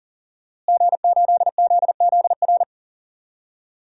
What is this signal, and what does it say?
Morse code, heard as a keyed tone, about 39 wpm, 700 Hz. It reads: G8ZZR
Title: DXCC-100 Call Signs - 5 Characters - Repeated 3x - 40wpm